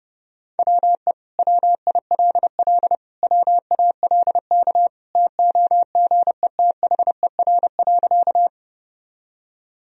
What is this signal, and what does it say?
Morse code, heard as a keyed tone, about 32 wpm, 700 Hz. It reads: WE WILL WALK TOGETHER.
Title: Sentences from Top 1000 Words - Review Part 1 - 30wpm